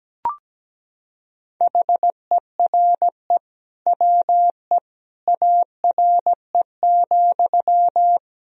Morse code, heard as a keyed tone, about 17 wpm, 700 Hz. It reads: HERE WE ARE,
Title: Sentences from Top 1000 Words - Review Part 3 - 17wpm